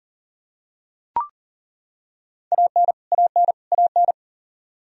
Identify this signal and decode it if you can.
Morse code, heard as a keyed tone, about 40 wpm, 700 Hz. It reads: AN AN AN
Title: Top 100 Words Repeated 40wpm